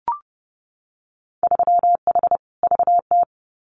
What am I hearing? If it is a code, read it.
Morse code, heard as a keyed tone, about 30 wpm, 700 Hz. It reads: 3H VT
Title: Winter Field Day Exchanges Encore 30wpm